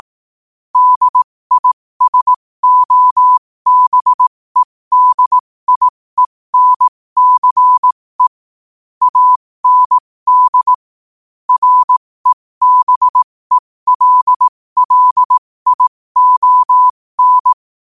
Morse code, heard as a keyed tone, about 18 wpm, spaced as for 15 wpm, 1000 Hz. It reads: DISOBEDIENCE AND REBELLION
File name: qotdfiles / 15wpm